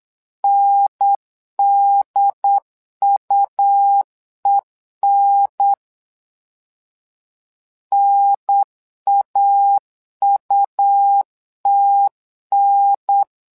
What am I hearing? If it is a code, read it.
Morse code, heard as a keyed tone, about 9 wpm, 800 Hz. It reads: NDUEN NAUTN
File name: LEZIONI / 2 UHDN / 2 UHDN 8wpm2232